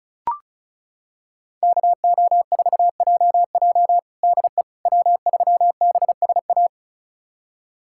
Morse code, heard as a keyed tone, about 34 wpm, 700 Hz. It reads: KO4JJ DE W3BSA
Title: USA Callsign DE Callsign - 5 Characters - 35wpm